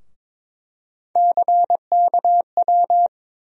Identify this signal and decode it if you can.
Morse code, heard as a keyed tone, about 22 wpm, 700 Hz. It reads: CKW
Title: USA Call Sign Postfixes - 3 Letters 22wpm